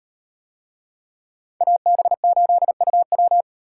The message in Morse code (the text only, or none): AB8UW